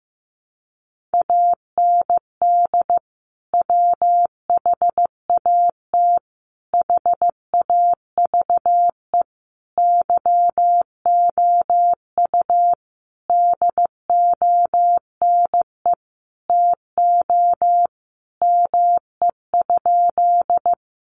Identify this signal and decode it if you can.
Morse code, heard as a keyed tone, about 15 wpm, 700 Hz. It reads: AND WHAT HAVE YOU DONE TO ME?
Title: Sentences from Top 500 Words 15wpm